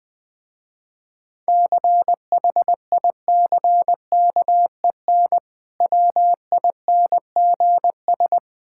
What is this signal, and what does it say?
Morse code, heard as a keyed tone, about 20 wpm, 700 Hz. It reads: CHICKEN WINGS
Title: Difficult Words with Hints - 20wpm